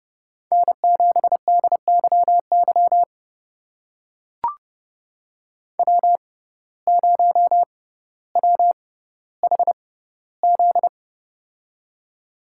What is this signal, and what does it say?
Morse code, heard as a keyed tone, about 30 wpm, spaced as for 15 wpm, 700 Hz. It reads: N7DYY W0WHZ
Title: USA Call Signs - 5 Characters - 6x Character Spacing - 30wpm